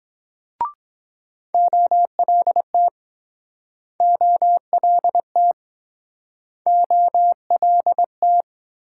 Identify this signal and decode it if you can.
Morse code, heard as a keyed tone, about 23 wpm, 700 Hz. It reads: OLT OLT OLT